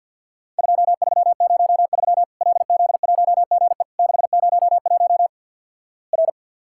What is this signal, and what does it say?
Morse code, heard as a keyed tone, about 52 wpm, 700 Hz. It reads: 2303 LZ1GE 601 R
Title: Worked All Europe DX Contest - QTC Exchange - 50wpm